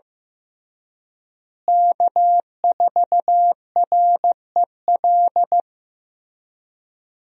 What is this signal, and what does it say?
Morse code, heard as a keyed tone, about 15 wpm, 700 Hz. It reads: K4REL
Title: USA Call Signs - 5 Characters - Repeated 3x - 15wpm